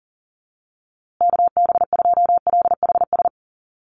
Morse code, heard as a keyed tone, about 40 wpm, 700 Hz. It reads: KB2LHS
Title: USA Call Signs Repeated 3x 40wpm